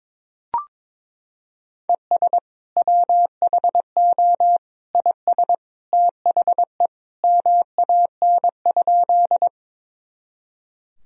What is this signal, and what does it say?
Morse code, heard as a keyed tone, about 22 wpm, 700 Hz. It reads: ES WHO IS THE MAN?
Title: Sentences from Top 200 Words and Abbreviations 22wpm